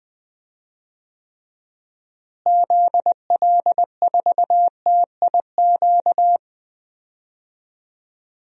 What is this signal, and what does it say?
Morse code, heard as a keyed tone, about 21 wpm, 700 Hz. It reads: ZL4TIQ
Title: DXCC-100 Call Signs - 6 Characters - Repeated 3x - 20wpm